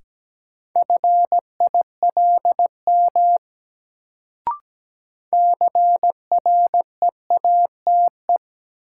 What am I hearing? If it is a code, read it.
Morse code, heard as a keyed tone, about 17 wpm, 700 Hz. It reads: FILM CREATE
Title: Sentences from Top 1000 Words - Review Part 3 - New Word Drill - 17wpm